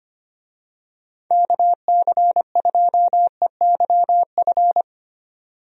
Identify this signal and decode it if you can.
Morse code, heard as a keyed tone, about 25 wpm, 700 Hz. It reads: KC2EYF
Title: USA Call Signs - 6 Characters - 25wpm